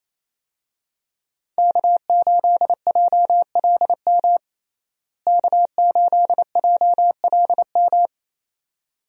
Morse code, heard as a keyed tone, about 28 wpm, 700 Hz. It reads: K8JLM K8JLM
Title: USA Call Signs - 5 Characters - Repeated 2x - 28wpm